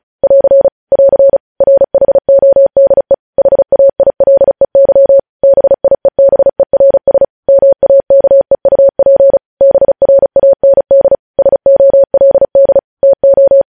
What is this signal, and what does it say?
Morse code, heard as a keyed tone, about 36 wpm, 550 Hz. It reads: <AR> <AR> RHODE HAILEY BIEBERS MAKEUP BRAND SOLD TO
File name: podcast / News headlines-1x-35x35